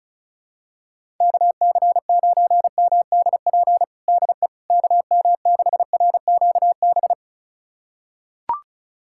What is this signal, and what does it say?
Morse code, heard as a keyed tone, about 36 wpm, 690 Hz. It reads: KC9MDP DE KM6RQB
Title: USA Callsign DE Callsign - 6 Characters - 35wpm